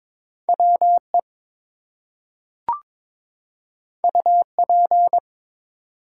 Morse code, heard as a keyed tone, about 22 wpm, 700 Hz. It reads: WE UP
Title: Two Letter Words - 22wpm